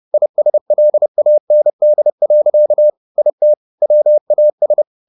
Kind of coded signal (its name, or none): Morse code